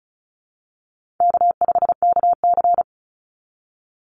Morse code, heard as a keyed tone, about 34 wpm, 700 Hz. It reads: K5KC
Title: Easy Call Signs Repeated 2x - 35wpm